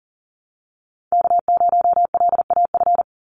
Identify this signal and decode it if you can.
Morse code, heard as a keyed tone, about 41 wpm, 700 Hz. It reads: K0LAF
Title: NA SOTA Chaser Callsigns 40wpm